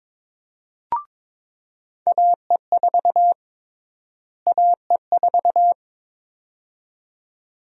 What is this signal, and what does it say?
Morse code, heard as a keyed tone, about 22 wpm, 700 Hz. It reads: AE4 AE4
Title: USA Call Sign Prefixes - Repeated 2x - 22wpm